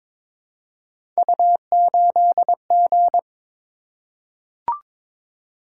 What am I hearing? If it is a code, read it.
Morse code, heard as a keyed tone, about 22 wpm, 700 Hz. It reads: U8G
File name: practice / 22wpm / Three Letter-Number Combinations - 22wpm